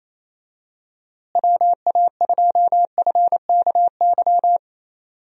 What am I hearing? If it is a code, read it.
Morse code, heard as a keyed tone, about 28 wpm, 700 Hz. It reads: WA2FKY
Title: USA Call Signs - 6 Characters - Repeated 3x - 28wpm